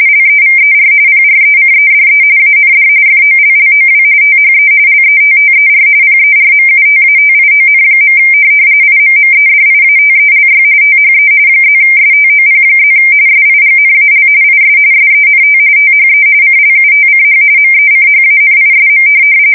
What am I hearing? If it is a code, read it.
RTTY, about 45.45 baud. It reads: LEVER PRECEDE AND FOLLOW THE LETTER PERMUTATION. IMMEDIATELY THE +STOP+ SIGNAL IS SENT THE CLUTCH MECHANISM
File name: fsk170 / morkrum